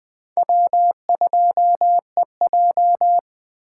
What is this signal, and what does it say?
Morse code, heard as a keyed tone, about 20 wpm, 700 Hz. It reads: W2EJ